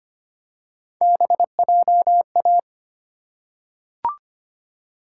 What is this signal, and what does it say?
Morse code, heard as a keyed tone, about 25 wpm, 700 Hz. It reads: BJA
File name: Three Letter-Number Combinations - 25wpm